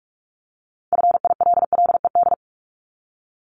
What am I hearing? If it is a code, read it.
Morse code, heard as a keyed tone, about 46 wpm, 700 Hz. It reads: FILLED
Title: Multi-letter Phonemes 45wpm